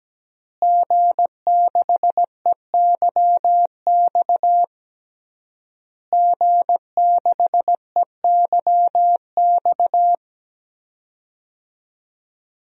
Morse code, heard as a keyed tone, about 17 wpm, 700 Hz. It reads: G6EYX G6EYX
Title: DXCC-100 Call Signs - 5 Characters - Repeated 3x - 17wpm